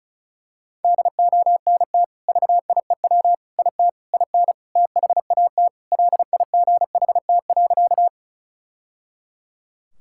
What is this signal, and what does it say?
Morse code, heard as a keyed tone, about 34 wpm, 700 Hz. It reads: DONT VIEW IT IN THAT LIGHT.
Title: Sentences from Top 1000 Words - Review Part 6 - 35wpm